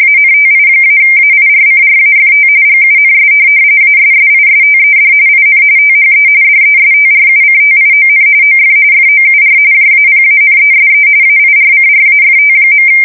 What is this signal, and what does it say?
RTTY, about 45.45 baud. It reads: ANTI-AIRCRAFT GUNS ASSISTED IN THE DESTRUCTION OF FOUR JAPANESE AIRCRAFT.
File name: fsk170 / USS-Missouri